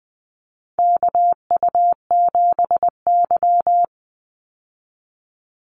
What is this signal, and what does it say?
Morse code, heard as a keyed tone, about 20 wpm, 700 Hz. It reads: KU7Y
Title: Call Signs Repeated 2x - 20wpm